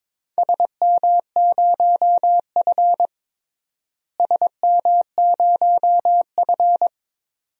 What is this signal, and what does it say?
Morse code, heard as a keyed tone, about 22 wpm, 700 Hz. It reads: SM0F SM0F
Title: DXCC-100 Call Signs - 4 Characters - Repeated 2x - 22wpm